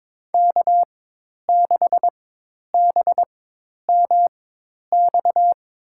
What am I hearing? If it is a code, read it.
Morse code, heard as a keyed tone, about 22 wpm, spaced as for 10 wpm, 700 Hz. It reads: K6BMX
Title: USA Call Signs - 4x Character Spacing - 22wpm